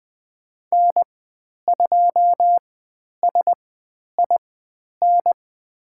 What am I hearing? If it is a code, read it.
Morse code, heard as a keyed tone, about 20 wpm, spaced as for 10 wpm, 700 Hz. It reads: N2SIN
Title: 102 Lesson 17g - Letter u - Call Signs 20wpm